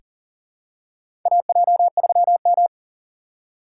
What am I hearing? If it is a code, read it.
Morse code, heard as a keyed tone, about 40 wpm, 690 Hz. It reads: AJ3M
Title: USA Call Signs - 4 Characters - 2x Character Spacing - 40wpm